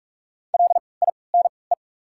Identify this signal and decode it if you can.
Morse code, heard as a keyed tone, about 44 wpm, spaced as for 25 wpm, 700 Hz. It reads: LINE
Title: 3x Character Spacing - Top 200 Words - 45wpm